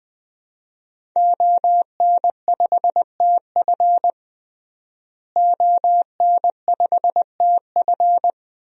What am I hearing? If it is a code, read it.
Morse code, heard as a keyed tone, about 20 wpm, 700 Hz. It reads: ON5TF ON5TF
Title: DXCC-100 Call Signs - 5 Characters - Repeated 2x - 20wpm